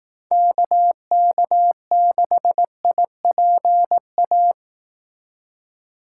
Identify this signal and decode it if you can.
Morse code, heard as a keyed tone, about 18 wpm, 700 Hz. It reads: KK6IPA
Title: Speed Racing - USA 2x3 Call Signs 15wpm